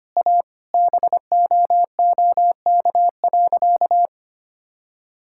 Morse code, heard as a keyed tone, about 25 wpm, 700 Hz. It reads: A BOOK.
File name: practice / 25wpm / Sentences with 2-Word Phrases - 25wpm